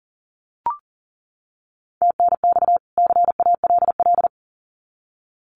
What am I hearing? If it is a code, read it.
Morse code, heard as a keyed tone, about 40 wpm, 700 Hz. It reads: TNX CALL
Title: Common QSO Phrases 40wpm